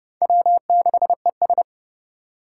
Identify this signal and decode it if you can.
Morse code, heard as a keyed tone, about 30 wpm, 700 Hz. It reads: W6ES
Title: Four Challenging Letter-Number Combinations 30wpm